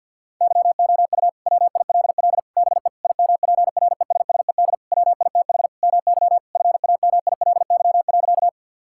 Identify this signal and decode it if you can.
Morse code, heard as a keyed tone, about 50 wpm, 700 Hz. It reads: YOU WILL BE IMPRESSED WITH MY FAMILY.
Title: Sentences from Top 1600 Words 50wpm